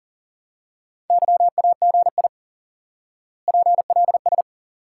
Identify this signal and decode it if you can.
Morse code, heard as a keyed tone, about 39 wpm, 700 Hz. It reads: YAGI PLS
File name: IQE Course 3 - Lesson 89 - YAGI - Two Elements - 40wpm